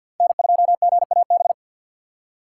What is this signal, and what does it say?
Morse code, heard as a keyed tone, about 64 wpm, 700 Hz. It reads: N2GAB